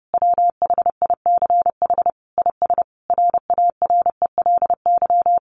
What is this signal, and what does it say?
Morse code, heard as a keyed tone, about 30 wpm, 700 Hz. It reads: WHICH IS RARELY
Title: Sentences from Top 1400 Words 30wpm